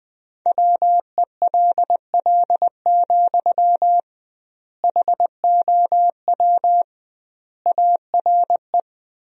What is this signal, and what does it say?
Morse code, heard as a keyed tone, about 19 wpm, 700 Hz. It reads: WELL, HOW ARE
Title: 2x Word-Spacing - Sentences from Top 200 Words - 20wpm